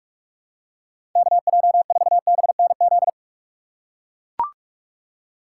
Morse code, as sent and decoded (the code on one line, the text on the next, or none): -.- .--- ....- -... -. --..
KJ4BNZ